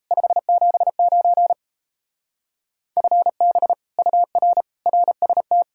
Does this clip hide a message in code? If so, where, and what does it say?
Yes: Morse code from 0.1 to 5.6 s, heard as a keyed tone, about 38 wpm, 700 Hz. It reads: 579 FB UR RST